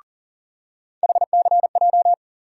Morse code, heard as a keyed tone, about 40 wpm, 700 Hz. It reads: HCJ
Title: USA Call Sign Postfixes - 3 Letters 40wpm